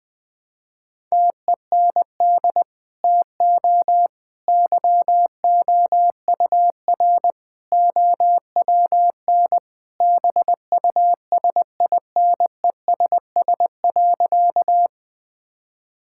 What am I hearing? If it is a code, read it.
Morse code, heard as a keyed tone, about 20 wpm, 700 Hz. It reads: TEND TO YOUR OWN BUSINESS.
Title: Sentences from Top 1000 Words - Review Part 5 - 20wpm